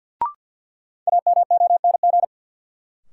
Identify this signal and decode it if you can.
Morse code, heard as a keyed tone, about 49 wpm, 700 Hz. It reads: AMONG
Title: Sentences from Top 500 Words - New Word Drill - 50wpm